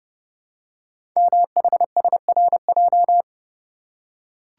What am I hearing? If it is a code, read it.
Morse code, heard as a keyed tone, about 30 wpm, 700 Hz. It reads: MHSRJ
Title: Code Groups - 5 Letters - 30wpm